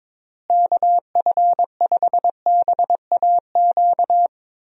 Morse code, heard as a keyed tone, about 22 wpm, 700 Hz. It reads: KF5BAQ
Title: USA Call Signs - 6 Characters - Repeated 2x - 22wpm